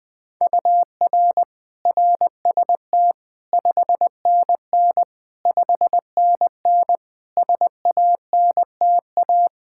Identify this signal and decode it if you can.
Morse code, heard as a keyed tone, about 20 wpm, 700 Hz. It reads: UR RST 5NN 5NN SANTA
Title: Practice QSO - Exchange 1 - Beginning - 20wpm